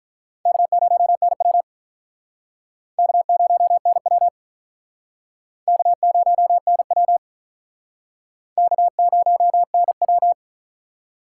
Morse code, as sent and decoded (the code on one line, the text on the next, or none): -.- ----- -. .-- / -.- ----- -. .-- / -.- ----- -. .-- / -.- ----- -. .--
K0NW K0NW K0NW K0NW